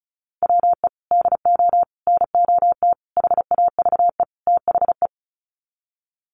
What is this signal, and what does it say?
Morse code, heard as a keyed tone, about 34 wpm, 700 Hz. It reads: WE DO NOT HAVE THE
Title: Sets of 5 Words 35wpm